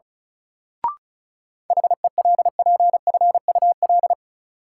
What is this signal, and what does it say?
Morse code, heard as a keyed tone, about 34 wpm, 700 Hz. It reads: HELPFUL
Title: Sentences from Top 1000 Words - Review Part 6 - New Word Drill - 35wpm